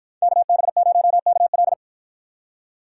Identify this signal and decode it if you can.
Morse code, heard as a keyed tone, about 50 wpm, 700 Hz. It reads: KB0KL